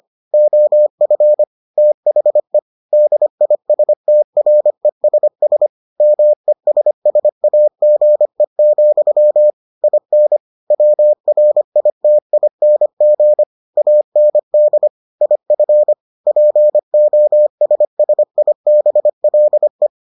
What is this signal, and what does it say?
Morse code, heard as a keyed tone, about 25 wpm, 600 Hz. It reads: OF THE DISTRESS MESSAGE, IN WRITING AND IF POSSIBLE